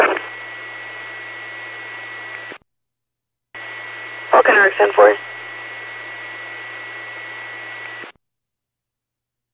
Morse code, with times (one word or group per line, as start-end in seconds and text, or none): none